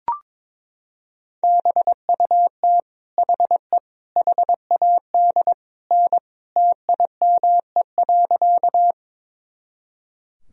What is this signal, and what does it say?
Morse code, heard as a keyed tone, about 22 wpm, 700 Hz. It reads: BUT HE HAD N TIME.
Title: Sentences from Top 100 Words and Abbreviations 22wpm